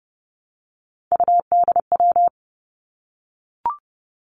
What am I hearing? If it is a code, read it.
Morse code, heard as a keyed tone, about 32 wpm, 700 Hz. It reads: UDW
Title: Three Challenging Letter-Number Combinations 30wpm